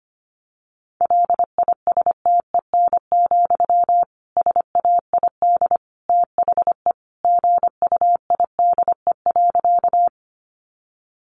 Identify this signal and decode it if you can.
Morse code, heard as a keyed tone, about 25 wpm, 700 Hz. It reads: LISTEN, SAID THE GUIDE.
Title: Sentences from Top 600 Words 25wpm